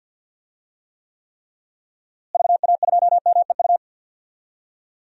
Morse code, heard as a keyed tone, about 52 wpm, 700 Hz. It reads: VA2MEU
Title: DXCC-100 Call Signs - 6 Characters - Repeated 3x - 50wpm